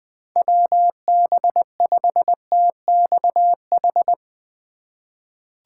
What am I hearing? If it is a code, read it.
Morse code, heard as a keyed tone, about 20 wpm, 700 Hz. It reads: WB5TXH
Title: USA Call Signs - 6 Characters - 2x Character Spacing - 20wpm